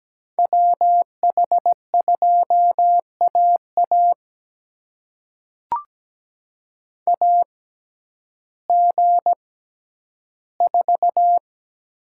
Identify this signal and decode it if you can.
Morse code, heard as a keyed tone, about 17 wpm, spaced as for 10 wpm, 700 Hz. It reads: WH2AA AG4
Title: USA Call Signs - 5 Characters - 6x Character Spacing - 17wpm